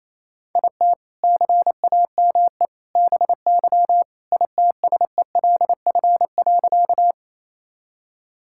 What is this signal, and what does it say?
Morse code, heard as a keyed tone, about 27 wpm, 700 Hz. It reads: IT CAME BY ITSELF.
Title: Sentences from Top 1000 Words - Review Part 5 - 28wpm